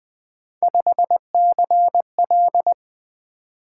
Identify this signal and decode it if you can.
Morse code, heard as a keyed tone, about 20 wpm, 700 Hz. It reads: 5CL
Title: Three Letter-Number Combinations - 20wpm